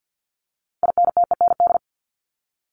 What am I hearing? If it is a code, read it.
Morse code, heard as a keyed tone, about 50 wpm, 700 Hz. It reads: INTEND
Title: ED Suffix - Top 100 Words with Base Words 50wpm